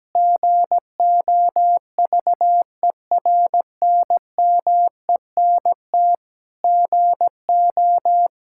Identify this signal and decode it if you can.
Morse code, heard as a keyed tone, about 17 wpm, 700 Hz. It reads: GOVERNMENT GO
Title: Sentences from Top 1000 Words - Review Part 5 - 17wpm